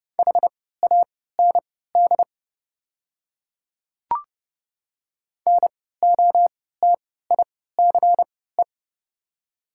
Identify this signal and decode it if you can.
Morse code, heard as a keyed tone, about 32 wpm, spaced as for 17 wpm, 700 Hz. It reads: HAND NOTICE
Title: 3x Character Spacing - Top 500 Words - 30wpm